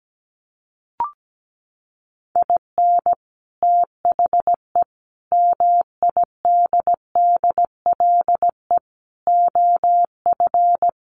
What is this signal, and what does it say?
Morse code, heard as a keyed tone, about 17 wpm, 700 Hz. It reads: IN THE MIDDLE OF
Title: Sets of 5 Words 17wpm